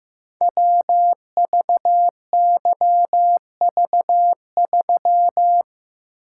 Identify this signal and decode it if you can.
Morse code, heard as a keyed tone, about 15 wpm, 700 Hz. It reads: WVYV3
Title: Code Groups - 5 Letter:Numbers - 15wpm